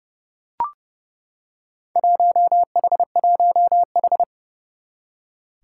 Morse code, heard as a keyed tone, about 29 wpm, 700 Hz. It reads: 1H1H
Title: Four Challenging Letter-Number Combinations 30wpm